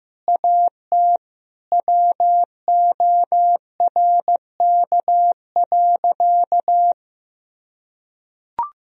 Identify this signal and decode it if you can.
Morse code, heard as a keyed tone, about 16 wpm, 700 Hz. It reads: AT WORK.
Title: Sentences from Top 1000 Words - Review Part 4 - 15wpm